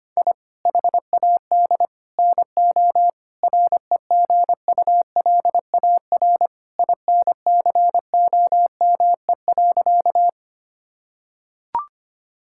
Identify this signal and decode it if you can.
Morse code, heard as a keyed tone, about 25 wpm, 700 Hz. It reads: I HAD NO REGULAR INCOME.
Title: Sentences from Top 1000 Words - Review Part 6 - 25wpm